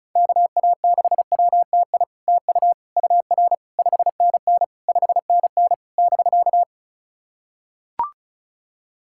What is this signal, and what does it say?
Morse code, heard as a keyed tone, about 36 wpm, 700 Hz. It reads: KA6WTI TU UR 5NN 5NN <BK>
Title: SOTA QSO Practice 35wpm